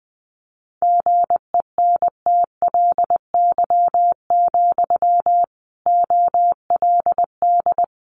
Morse code, heard as a keyed tone, about 20 wpm, 700 Hz. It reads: GENTLY, OLD
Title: Sentences from Top 1500 Words 20wpm